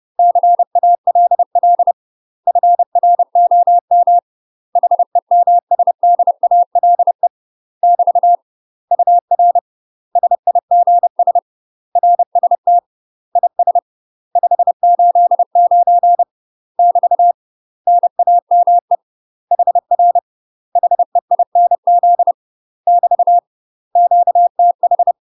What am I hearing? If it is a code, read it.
Morse code, heard as a keyed tone, about 30 wpm, 700 Hz. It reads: CALL FROM HEMSDALE <BT> UR SIGS RST IS 589 <BT> NAME HR HEINZ <BT> QTH
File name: UK MASTER QSOs TRAINING / 30WPM / UK MASTER QSOs 0040